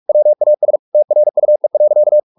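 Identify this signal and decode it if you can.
Morse code, heard as a keyed tone, about 44 wpm, 600 Hz. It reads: WAS TRUE.
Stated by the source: Morse practice at 45 wpm, the novel Never by Ken Follett in English, from CW med Gnister.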